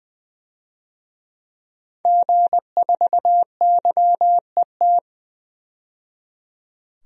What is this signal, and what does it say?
Morse code, heard as a keyed tone, about 20 wpm, 700 Hz. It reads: G4YET
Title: DXCC-100 Call Signs - 5 Characters - 20wpm